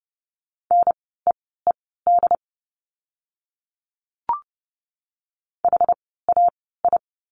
Morse code, heard as a keyed tone, about 30 wpm, spaced as for 17 wpm, 700 Hz. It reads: NEED HAI
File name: Lesson 17f - Letter u - Words 30wpm